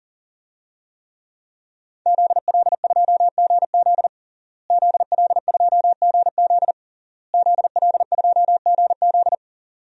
Morse code, heard as a keyed tone, about 40 wpm, 700 Hz. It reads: ZL2GZ ZL2GZ ZL2GZ
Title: DXCC-100 Call Signs - 5 Characters - Repeated 3x - 40wpm